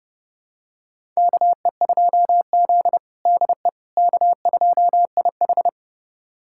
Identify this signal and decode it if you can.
Morse code, heard as a keyed tone, about 30 wpm, 700 Hz. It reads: KE2Z DE K2IH
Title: USA Callsign DE Callsign - 4 Characters - 30wpm